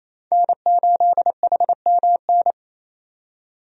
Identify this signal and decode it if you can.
Morse code, heard as a keyed tone, about 28 wpm, 700 Hz. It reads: N8HMN
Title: USA Call Signs - 5 Characters - 28wpm